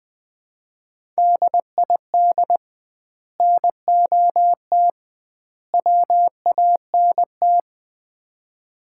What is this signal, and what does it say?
Morse code, heard as a keyed tone, about 20 wpm, 700 Hz. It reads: DID NOT WANT
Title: 2x Word-Spacing - Sets of 3 Words - 20wpm